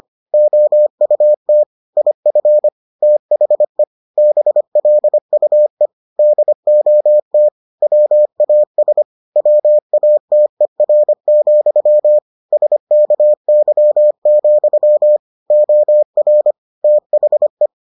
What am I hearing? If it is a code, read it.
Morse code, heard as a keyed tone, about 25 wpm, 600 Hz. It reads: OUT IF THE BLUE DOT WAS WATER, SKY, OR THE